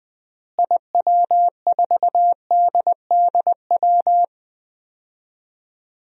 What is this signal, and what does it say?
Morse code, heard as a keyed tone, about 20 wpm, 700 Hz. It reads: IW4DDW
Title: DXCC-100 Call Signs - 6 Characters - Repeated 2x - 20wpm